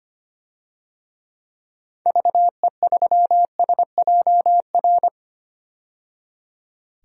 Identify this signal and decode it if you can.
Morse code, heard as a keyed tone, about 25 wpm, 700 Hz. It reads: VE3SJR
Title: DXCC-100 Call Signs - 6 Characters - 25wpm